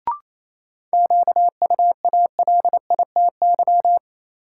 Morse code, heard as a keyed tone, about 28 wpm, 700 Hz. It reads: QUALITY
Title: Sentences from Top 1000 Words - Review Part 4 - New Word Drill - 28wpm